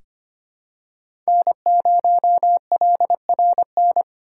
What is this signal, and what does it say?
Morse code, heard as a keyed tone, about 25 wpm, 700 Hz. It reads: N0LRN
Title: USA Call Signs - 8x Character Spacing - 25wpm